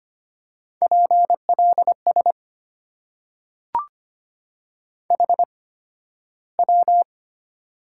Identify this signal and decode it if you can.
Morse code, heard as a keyed tone, about 25 wpm, spaced as for 13 wpm, 700 Hz. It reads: PLS HW
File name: Top 90 QSO Elements - 8x Character Spacing - 25wpm